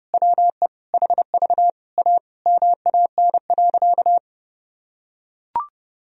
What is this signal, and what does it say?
Morse code, heard as a keyed tone, about 28 wpm, 700 Hz. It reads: WE HV A MAN.
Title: Sentences from Top 300 Words and Abbreviations 30wpm